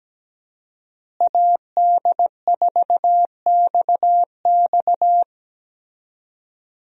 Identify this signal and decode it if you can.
Morse code, heard as a keyed tone, about 17 wpm, 700 Hz. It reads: AD4XX
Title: USA Call Signs - 5 Characters - Repeated 3x - 17wpm